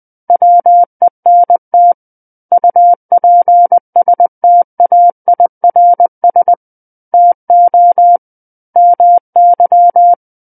Morse code, heard as a keyed tone, about 20 wpm, 700 Hz. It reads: WENT UPSTAIRS TO MY
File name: 20 WPM